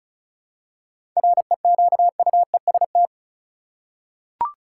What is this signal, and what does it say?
Morse code, heard as a keyed tone, about 36 wpm, 700 Hz. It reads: REQUEST
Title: Seven Letter Words - 35wpm